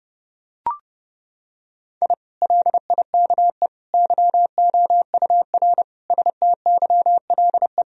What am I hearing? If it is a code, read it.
Morse code, heard as a keyed tone, about 30 wpm, 700 Hz. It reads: I LIKE YOUR STYLE
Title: Sentences from Top 1000 Words - Review Part 4 - 30wpm